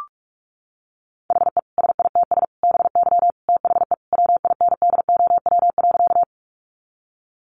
Morse code, heard as a keyed tone, about 44 wpm, 700 Hz. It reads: HE SITS BY THE WINDOW.
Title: Easy Phrases 45wpm